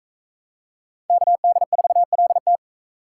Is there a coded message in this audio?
Yes: Morse code from 1.1 to 2.6 s, heard as a keyed tone, about 44 wpm, 700 Hz. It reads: KD4LT